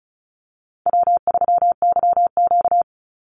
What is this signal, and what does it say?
Morse code, heard as a keyed tone, about 34 wpm, 700 Hz. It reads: W3YQ
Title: Easy Call Signs Repeated 3x - 35wpm